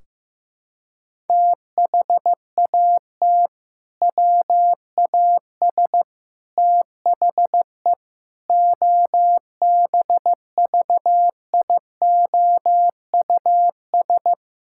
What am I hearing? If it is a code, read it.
Morse code, heard as a keyed tone, about 15 wpm, 700 Hz. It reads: THAT WAS THE OBVIOUS
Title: Sentences from Top 1600 Words 15wpm